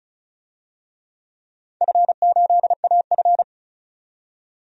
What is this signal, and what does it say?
Morse code, heard as a keyed tone, about 36 wpm, 700 Hz. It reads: F8AF
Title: DXCC-100 Call Signs - 4 Characters - 35wpm